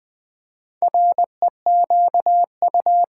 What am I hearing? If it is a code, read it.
Morse code, heard as a keyed tone, about 20 wpm, 700 Hz. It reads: REQU
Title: Build A Word - 4 Letters At A Time - 11-Letter Words - 20wpm